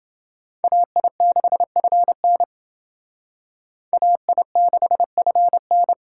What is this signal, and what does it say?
Morse code, heard as a keyed tone, about 30 wpm, 700 Hz. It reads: AI6FN AI6FN